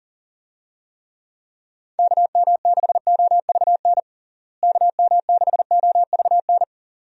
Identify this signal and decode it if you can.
Morse code, heard as a keyed tone, about 40 wpm, 700 Hz. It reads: KM6OVN KM6OVN
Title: USA Call Signs - 6 Characters - Repeated 2x - 40wpm